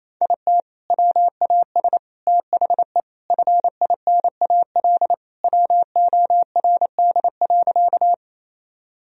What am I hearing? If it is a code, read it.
Morse code, heard as a keyed tone, about 28 wpm, 700 Hz. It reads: IT WAS THE FINAL WORD.
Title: Sentences from Top 1000 Words - Review Part 2 - 28wpm